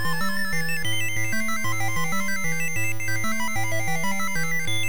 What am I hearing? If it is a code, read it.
There is no code.